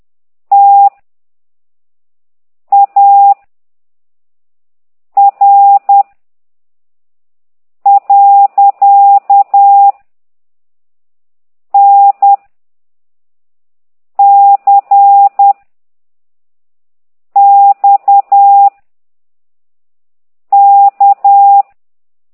Morse code, heard as a keyed tone, about 10 wpm, 800 Hz. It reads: T A R . N C X K
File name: exerc7